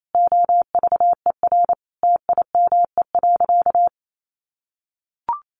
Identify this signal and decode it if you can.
Morse code, heard as a keyed tone, about 28 wpm, 700 Hz. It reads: OVER TIME.
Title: Sentences from Top 1500 Words 28wpm